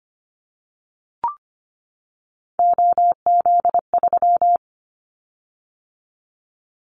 Morse code, heard as a keyed tone, about 25 wpm, 700 Hz. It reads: OZ3
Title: Callsign Prefixes for Top 50 DXCC - 25wpm